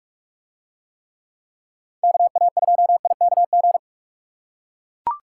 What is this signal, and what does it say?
Morse code, heard as a keyed tone, about 42 wpm, 700 Hz. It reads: KA2IKG